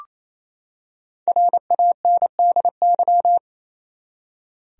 Morse code, heard as a keyed tone, about 28 wpm, 700 Hz. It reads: RANDY
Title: USA Common Names - Max 5 Letters - 28wpm